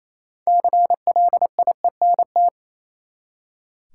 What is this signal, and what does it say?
Morse code, heard as a keyed tone, about 28 wpm, 700 Hz. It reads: CLIENT
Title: Six Letter Words - 28wpm